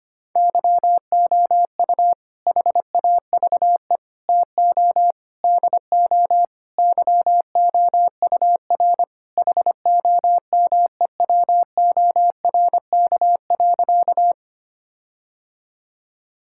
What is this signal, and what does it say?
Morse code, heard as a keyed tone, about 25 wpm, 700 Hz. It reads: YOU HAVE TO DO YOUR HOMEWORK.
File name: practice / 25wpm / Sentences with 4-Word Phrases - 25wpm